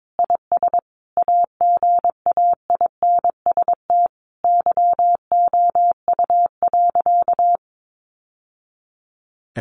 Morse code, heard as a keyed tone, about 22 wpm, 700 Hz. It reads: IS AGAINST YOU.
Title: Sentences from Top 700 Words 22wpm